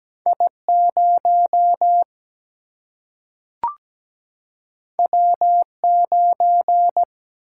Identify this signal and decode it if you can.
Morse code, heard as a keyed tone, about 17 wpm, 700 Hz. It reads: I 0 W 9
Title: Single-Letter Single-Number 17wpm